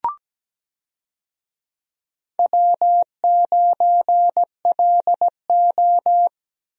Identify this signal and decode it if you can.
Morse code, heard as a keyed tone, about 17 wpm, 700 Hz. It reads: W9LO
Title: USA Call Signs - 4 Characters - Repeated 2x - 17wpm